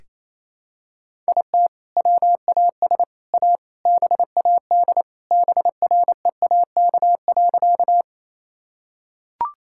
Morse code, heard as a keyed tone, about 28 wpm, 700 Hz. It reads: IT WAS A BAD BREAK.
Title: Sentences from Top 1000 Words - Review Part 6 - 28wpm